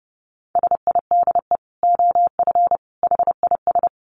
Morse code, heard as a keyed tone, about 30 wpm, 700 Hz. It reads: SIDE OF HIS
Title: Dit Challenge 30wpm